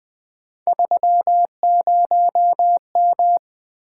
Morse code, heard as a keyed tone, about 20 wpm, 700 Hz. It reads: 30M
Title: Three Letter-Number Combinations - 20wpm